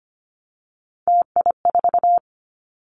Morse code, heard as a keyed tone, about 25 wpm, 700 Hz. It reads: TI4
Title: Callsign Prefixes for Top 100 DXCC - 25wpm